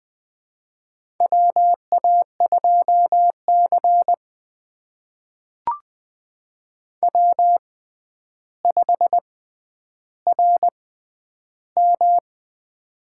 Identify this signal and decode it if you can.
Morse code, heard as a keyed tone, about 20 wpm, spaced as for 10 wpm, 700 Hz. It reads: WA2C W5RM
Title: USA Call Signs - 4 Characters - 6x Character Spacing - 20wpm